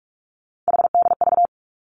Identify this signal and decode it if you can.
Morse code, heard as a keyed tone, about 46 wpm, 700 Hz. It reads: HDV
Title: Three Challenging Letter-Number Combinations 45wpm